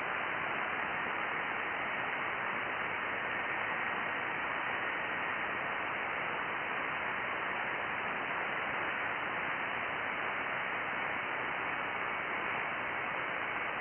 FT8.
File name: IC-7100